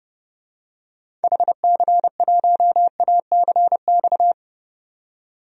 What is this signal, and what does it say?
Morse code, heard as a keyed tone, about 31 wpm, 700 Hz. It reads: HC1ACX
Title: DXCC-100 Call Signs - 6 Characters - 30wpm